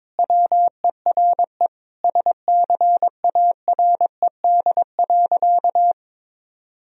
Morse code, heard as a keyed tone, about 22 wpm, 700 Hz. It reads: WERE SCARED.
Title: Sentences from Top 1000 Words - Review Part 4 - 22wpm